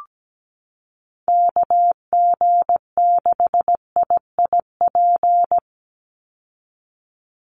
Morse code, heard as a keyed tone, about 17 wpm, 700 Hz. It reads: KG6IIP
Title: Call Signs Repeated 2x - 17wpm